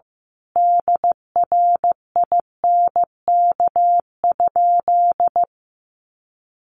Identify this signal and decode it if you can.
Morse code, heard as a keyed tone, about 15 wpm, 700 Hz. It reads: DRINK?
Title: Sentences from Top 1500 Words 15wpm